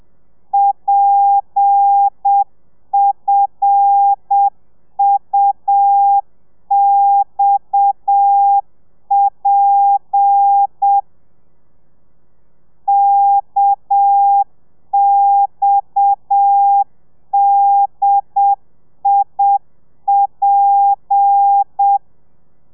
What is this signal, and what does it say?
Morse code, heard as a keyed tone, about 7 wpm, 800 Hz. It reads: PFUXP KXDIP
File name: telegrafia / i2tfj / Lez7 30